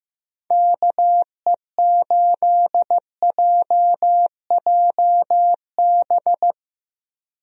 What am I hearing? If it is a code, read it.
Morse code, heard as a keyed tone, about 15 wpm, 700 Hz. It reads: KE8JJB
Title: USA Call Signs - 6 Characters - 15wpm